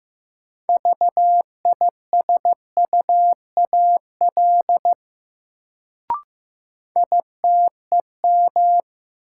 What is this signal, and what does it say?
Morse code, heard as a keyed tone, about 15 wpm, 700 Hz. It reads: VISUAL ITEM
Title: Sentences from Top 1000 Words - Review Part 6 - New Word Drill - 15wpm